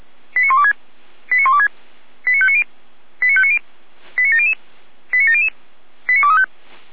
Five-tone selective call (ZVEI), heard as quick runs of five tones, tones of about 70 ms. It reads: 89126 89126 89590 89590 89700 89700 89235